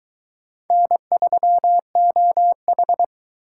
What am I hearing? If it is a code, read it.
Morse code, heard as a keyed tone, about 22 wpm, 700 Hz. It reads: N3OH